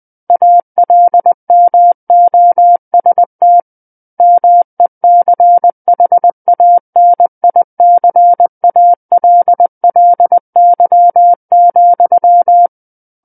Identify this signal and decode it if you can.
Morse code, heard as a keyed tone, about 20 wpm, 700 Hz. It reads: ALMOST MECHANICALLY,
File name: 20 WPM